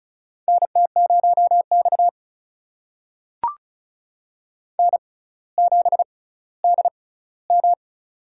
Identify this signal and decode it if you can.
Morse code, heard as a keyed tone, about 36 wpm, spaced as for 18 wpm, 700 Hz. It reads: NT0X N7DM
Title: USA Call Signs - 6x Character Spacing - 35wpm